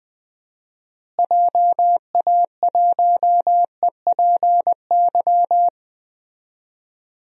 Morse code, heard as a keyed tone, about 20 wpm, 700 Hz. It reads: JA1EPY
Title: DXCC-100 Call Signs - 6 Characters - Repeated 2x - 20wpm